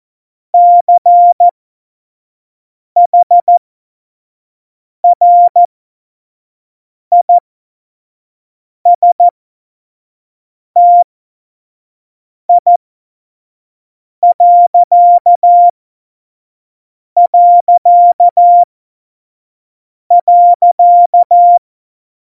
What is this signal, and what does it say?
Morse code, heard as a keyed tone, about 14 wpm, spaced as for 5 wpm, 700 Hz. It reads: CHRISTI...
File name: Morsekurs / Witze / Witz 05. Januar 2021, 25 F70